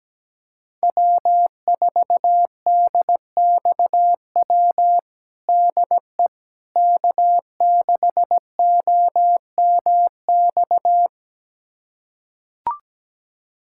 Morse code, heard as a keyed tone, about 17 wpm, 700 Hz. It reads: W4DXW DE K6OMX
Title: USA Callsign DE Callsign - 5 Characters - 17wpm